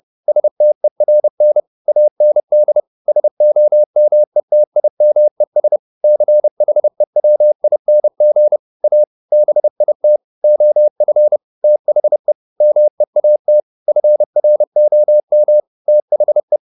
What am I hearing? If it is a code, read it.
Morse code, heard as a keyed tone, about 30 wpm, 600 Hz. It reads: STERN AND SOMETIMES CHEWING A BIT OF THE MEAT FROM THE